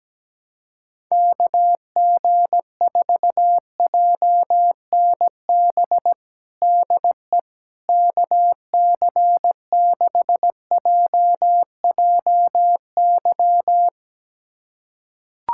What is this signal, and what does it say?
Morse code, heard as a keyed tone, about 17 wpm, 700 Hz. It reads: KG4JNB DE KC6JJY
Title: USA Callsign DE Callsign 17wpm